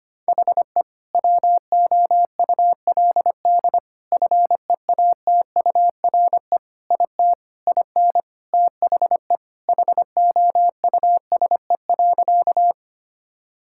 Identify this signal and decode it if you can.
Morse code, heard as a keyed tone, about 25 wpm, 700 Hz. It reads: HE WOULD FEATURE IT IN THE HOUSE.
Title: Sentences from Top 1000 Words - Review Part 5 - 25wpm